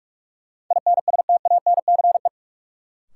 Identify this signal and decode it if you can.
Morse code, heard as a keyed tone, about 46 wpm, 700 Hz. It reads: INSTANCE
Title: Sentences from Top 800 Words - New Word Drill - 45wpm